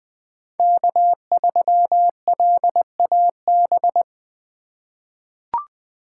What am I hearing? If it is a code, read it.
Morse code, heard as a keyed tone, about 20 wpm, 700 Hz. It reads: K3LAB
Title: USA Call Signs - 5 Characters - 2x Character Spacing - 20wpm